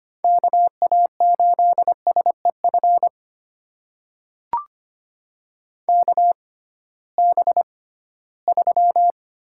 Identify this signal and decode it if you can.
Morse code, heard as a keyed tone, about 25 wpm, spaced as for 16 wpm, 700 Hz. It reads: KA8SEF KB3
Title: USA Call Signs - 6 Characters - 6x Character Spacing - 25wpm